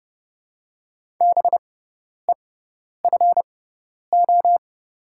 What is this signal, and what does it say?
Morse code, heard as a keyed tone, about 30 wpm, spaced as for 10 wpm, 700 Hz. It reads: BEFO
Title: 6x Character Spacing - Top 200 Words - 30wpm